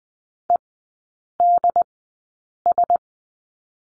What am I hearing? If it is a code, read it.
Morse code, heard as a keyed tone, about 20 wpm, spaced as for 8 wpm, 690 Hz. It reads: EDS
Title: 093 Lesson 16e - Letter c - Three Characters 20wpm